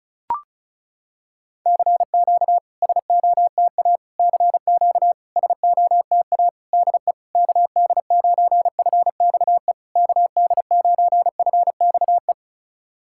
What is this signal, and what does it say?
Morse code, heard as a keyed tone, about 35 wpm, 700 Hz. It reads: CQ SOTA CQ SOTA DE KD9FXE KD9FXE
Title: SOTA QSO Practice 35wpm